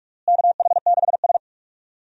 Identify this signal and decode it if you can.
Morse code, heard as a keyed tone, about 45 wpm, 700 Hz. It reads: KH6S
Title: DXCC-100 Call Signs - 4 Characters - Repeated 3x - 45wpm